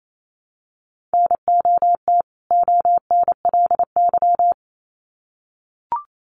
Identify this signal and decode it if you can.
Morse code, heard as a keyed tone, about 28 wpm, 700 Hz. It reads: NOT ONLY
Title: Sets of 2 Words 28wpm